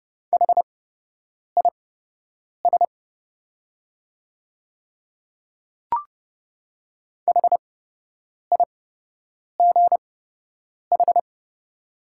Morse code, heard as a keyed tone, about 30 wpm, spaced as for 8 wpm, 690 Hz. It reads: HIS HIGH
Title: 8x Character Spacing - Top 200 Words - 30wpm